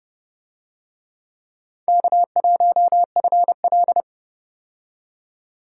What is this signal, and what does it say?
Morse code, heard as a keyed tone, about 31 wpm, 700 Hz. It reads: K1FL
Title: USA Call Signs - 4 Characters - 30wpm